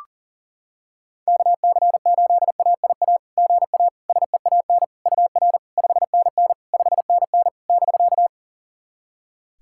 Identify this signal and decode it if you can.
Morse code, heard as a keyed tone, about 40 wpm, 690 Hz. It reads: KC8AIA GA SEAN UR 5NN 5NN <BK>
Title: SOTA QSO Practice 40wpm